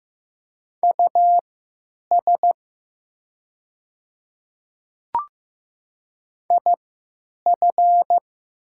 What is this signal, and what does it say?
Morse code, heard as a keyed tone, about 15 wpm, spaced as for 9 wpm, 700 Hz. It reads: US IF
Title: 3x Character Spacing - Top 200 Words - 15wpm